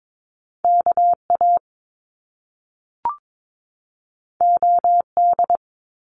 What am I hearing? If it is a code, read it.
Morse code, heard as a keyed tone, about 22 wpm, 700 Hz. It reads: KA OD
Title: Top Two Letter Combinations in Words 22wpm